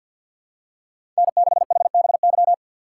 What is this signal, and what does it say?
Morse code, heard as a keyed tone, about 51 wpm, 700 Hz. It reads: N6HBY
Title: USA Call Signs - 6x Character Spacing - 50wpm